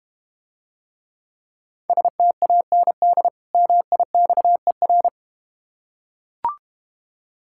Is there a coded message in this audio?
Yes: Morse code from 1.9 to 5.1 s, heard as a keyed tone, about 31 wpm, 700 Hz. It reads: STAND MIXER